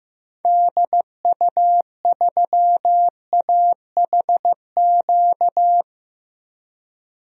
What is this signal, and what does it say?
Morse code, heard as a keyed tone, about 15 wpm, 700 Hz. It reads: DU3AHQ
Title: DXCC-100 Call Signs - 6 Characters - Repeated 2x - 15wpm